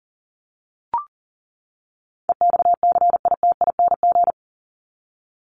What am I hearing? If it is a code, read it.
Morse code, heard as a keyed tone, about 40 wpm, 700 Hz. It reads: EXCITING
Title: Top 1751-2000 Words 40wpm